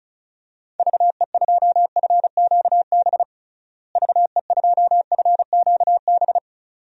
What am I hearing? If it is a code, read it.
Morse code, heard as a keyed tone, about 36 wpm, 700 Hz. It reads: VE2FQB VE2FQB
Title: DXCC-100 Call Signs - 6 Characters - Repeated 2x - 35wpm